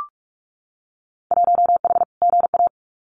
Morse code, heard as a keyed tone, about 46 wpm, 700 Hz. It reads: 1H GA
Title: Winter Field Day Exchanges 45wpm